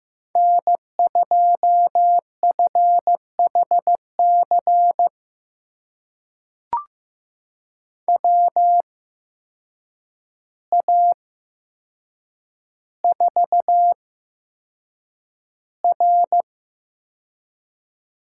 Morse code, heard as a keyed tone, about 15 wpm, spaced as for 7 wpm, 700 Hz. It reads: N2FHC WA4R
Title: USA Call Signs - 5 Characters - 8x Character Spacing - 15wpm